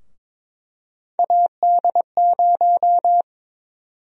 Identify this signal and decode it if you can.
Morse code, heard as a keyed tone, about 22 wpm, 700 Hz. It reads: AD0
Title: USA Call Sign Prefixes - Repeated 2x - 22wpm